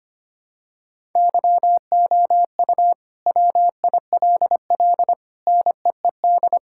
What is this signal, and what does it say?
Morse code, heard as a keyed tone, about 25 wpm, 700 Hz. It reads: YOU WILL NEED
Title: Sentences from Top 1000 Words - Review Part 5 - 25wpm